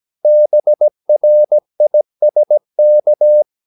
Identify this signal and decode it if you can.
Morse code, heard as a keyed tone, about 17 wpm, 600 Hz. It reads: BRISK
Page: Romanen Never af Ken Follet på engelsk.